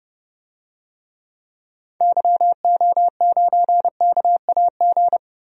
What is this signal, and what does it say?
Morse code, heard as a keyed tone, about 30 wpm, 700 Hz. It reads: YO9KAG
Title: DXCC-100 Call Signs - 6 Characters - Repeated 3x - 30wpm